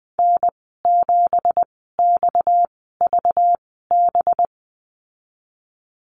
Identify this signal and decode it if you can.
Morse code, heard as a keyed tone, about 20 wpm, spaced as for 15 wpm, 700 Hz. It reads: N7XVB
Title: 238 Lesson 38g - Letter q - Call Signs 20wpm